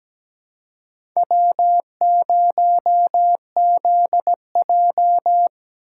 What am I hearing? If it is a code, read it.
Morse code, heard as a keyed tone, about 17 wpm, 700 Hz. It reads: W0ZJ
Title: USA Call Signs - 2x Character Spacing - 17wpm